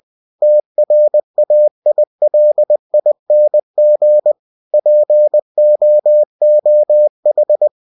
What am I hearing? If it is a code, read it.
Morse code, heard as a keyed tone, about 20 wpm, 600 Hz. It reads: TRAILING POOH